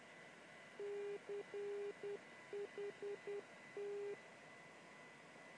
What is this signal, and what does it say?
Morse code, heard as a keyed tone, about 10 wpm, 410 Hz, with noise. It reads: CHT
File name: chiltern-non-directional-beacon-callsign-cht-in-morse-code-broadcasting-on-277-khz-recording-from-278-khz-on-lw